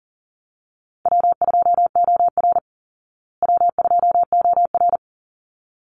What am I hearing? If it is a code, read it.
Morse code, heard as a keyed tone, about 40 wpm, 700 Hz. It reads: W2OR W2OR
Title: Easy Call Signs Repeated 2x - 40wpm